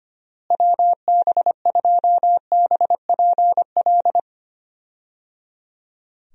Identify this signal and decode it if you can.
Morse code, heard as a keyed tone, about 25 wpm, 700 Hz. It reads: WB2BPL
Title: USA Call Signs - 6 Characters - Repeated 3x - 25wpm